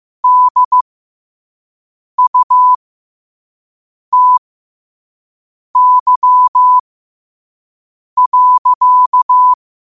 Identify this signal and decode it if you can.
Morse code, heard as a keyed tone, about 15 wpm, spaced as for 5 wpm, 1000 Hz. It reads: DUTY.
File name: qotdfiles / 5wpm